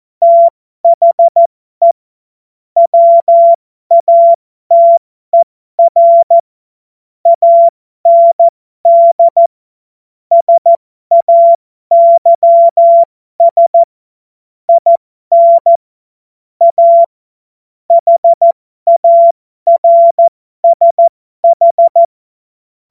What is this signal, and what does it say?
Morse code, heard as a keyed tone, about 14 wpm, spaced as for 12 wpm, 680 Hz. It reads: THE WATER AND SAYS IN A HARSH